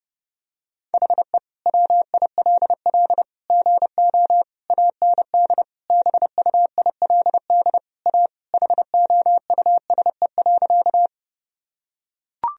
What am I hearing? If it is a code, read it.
Morse code, heard as a keyed tone, about 32 wpm, 700 Hz. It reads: HE WILL GO AND BUILD A HOUSE.
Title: Sentences from Top 1000 Words - Review Part 1 - 30wpm